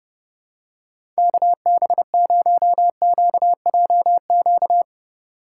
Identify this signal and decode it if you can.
Morse code, heard as a keyed tone, about 30 wpm, 690 Hz. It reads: KB0QJQ
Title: USA Call Signs - 3x Character Spacing - 30wpm